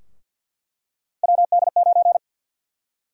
Morse code, heard as a keyed tone, about 49 wpm, 700 Hz. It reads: WD9
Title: USA Call Sign Prefixes - Repeated 2x - 50wpm